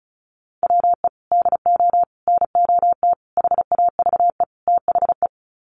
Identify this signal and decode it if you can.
Morse code, heard as a keyed tone, about 34 wpm, 700 Hz. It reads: WE DO NOT HAVE THE
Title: Sets of 5 Words 35wpm